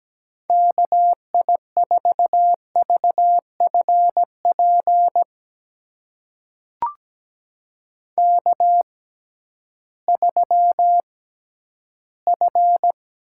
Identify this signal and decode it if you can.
Morse code, heard as a keyed tone, about 17 wpm, spaced as for 11 wpm, 700 Hz. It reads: KI4VFP K3F
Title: USA Call Signs - 6x Character Spacing - 17wpm